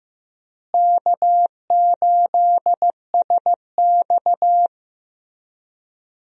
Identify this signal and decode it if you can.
Morse code, heard as a keyed tone, about 15 wpm, 700 Hz. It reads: K8SX
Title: USA Call Signs - 4 Characters - Repeated 3x - 15wpm